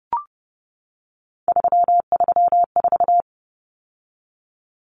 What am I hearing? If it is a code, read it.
Morse code, heard as a keyed tone, about 30 wpm, 700 Hz. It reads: 334
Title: Numbers - 3-Digit 30wpm